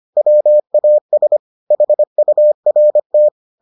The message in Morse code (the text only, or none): WAS HURT